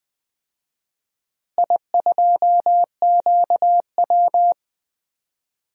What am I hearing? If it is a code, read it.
Morse code, heard as a keyed tone, about 20 wpm, 700 Hz. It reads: I2QW
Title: DXCC-100 Call Signs - 4 Characters - Repeated 3x - 20wpm